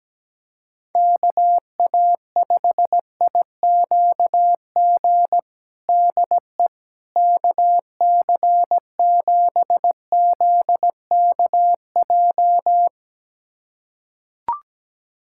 Morse code, heard as a keyed tone, about 17 wpm, 700 Hz. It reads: KA5IQG DE KC7ZKJ
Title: USA Callsign DE Callsign - 6 Characters - 17wpm